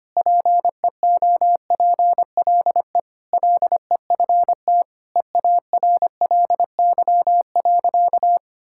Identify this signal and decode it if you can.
Morse code, heard as a keyed tone, about 25 wpm, 700 Hz. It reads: PEOPLE LEFT EARLY.
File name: practice / 25wpm / Sentences with 3-Word Phrases - 25wpm